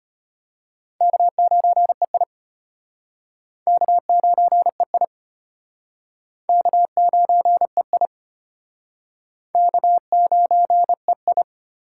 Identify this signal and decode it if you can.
Morse code, heard as a keyed tone, about 33 wpm, 700 Hz. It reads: K9EI K9EI K9EI K9EI